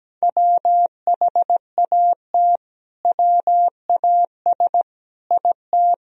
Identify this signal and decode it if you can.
Morse code, heard as a keyed tone, about 17 wpm, 700 Hz. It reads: WHAT WAS IT
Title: Sentences from Top 100 Words and Abbreviations 17wpm